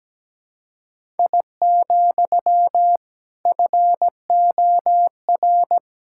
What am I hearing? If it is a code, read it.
Morse code, heard as a keyed tone, about 17 wpm, 700 Hz. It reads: I, FOR
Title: Sentences from Top 1000 Words - Review Part 5 - 17wpm